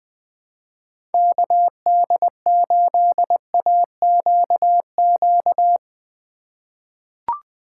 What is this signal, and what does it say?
Morse code, heard as a keyed tone, about 20 wpm, 700 Hz. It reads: KD8AQQ
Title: USA Call Signs - 6 Characters - 2x Character Spacing - 20wpm